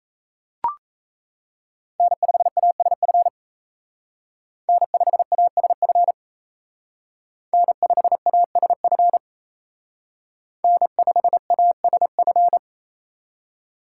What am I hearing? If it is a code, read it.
Morse code, heard as a keyed tone, about 36 wpm, 700 Hz. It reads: N5ASF N5ASF N5ASF N5ASF